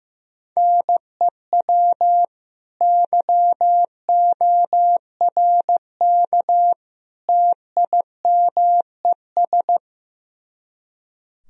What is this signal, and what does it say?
Morse code, heard as a keyed tone, about 15 wpm, 700 Hz. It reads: NEW YORK TIMES
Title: Speed-Racing - Sets of 3 Words 15wpm